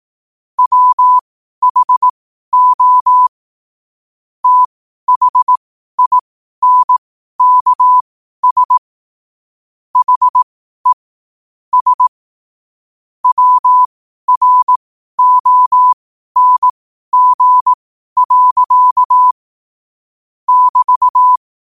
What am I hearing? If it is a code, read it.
Morse code, heard as a keyed tone, about 18 wpm, spaced as for 12 wpm, 1000 Hz. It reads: WHO THINKS HE S WRONG. <BT>